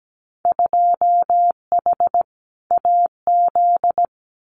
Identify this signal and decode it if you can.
Morse code, heard as a keyed tone, about 17 wpm, 700 Hz. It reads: 2H AZ
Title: Winter Field Day Exchanges 17wpm